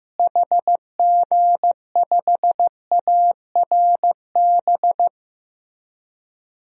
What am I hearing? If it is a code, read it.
Morse code, heard as a keyed tone, about 15 wpm, 700 Hz. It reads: HG5ARB
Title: DXCC-100 Call Signs - 6 Characters - Repeated 2x - 15wpm